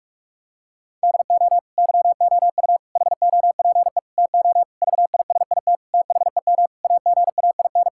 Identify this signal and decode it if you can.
Morse code, heard as a keyed tone, about 46 wpm, 700 Hz. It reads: DO YOU HOPE TO VISIT THEM AGAIN
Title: Sentences from Top 1000 Words - Review Part 6 - 45wpm